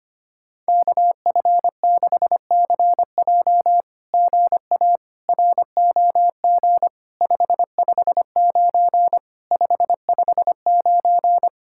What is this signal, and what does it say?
Morse code, heard as a keyed tone, about 26 wpm, 700 Hz. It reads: KF6CJ GA ROG 559 559
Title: SOTA QSO Practice 25wpm